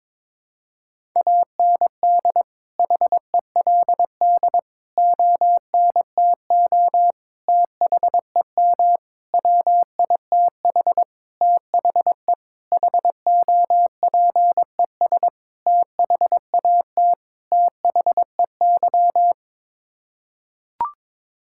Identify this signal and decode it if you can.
Morse code, heard as a keyed tone, about 22 wpm, 700 Hz. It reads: AND HELD ONTO THEM WITH THE HOPES THAT THEY
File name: Sets of 9 Words - 22wpm